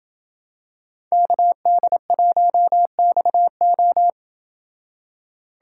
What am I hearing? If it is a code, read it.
Morse code, heard as a keyed tone, about 27 wpm, 700 Hz. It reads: KD1XO